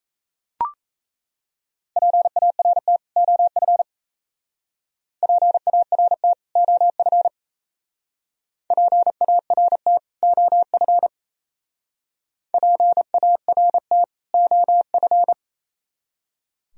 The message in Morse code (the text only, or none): PART OF PART OF PART OF PART OF